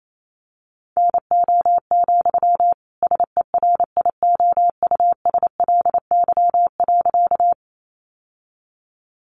Morse code, heard as a keyed tone, about 28 wpm, 700 Hz. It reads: NO, SERIOUSLY.
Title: Sentences from Top 1200 Words 28wpm